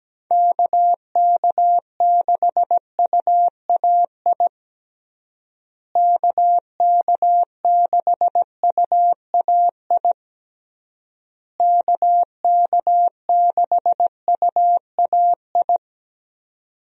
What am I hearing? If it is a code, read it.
Morse code, heard as a keyed tone, about 17 wpm, 700 Hz. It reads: KK6UAI KK6UAI KK6UAI
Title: USA Call Signs - 6 Characters - Repeated 3x - 17wpm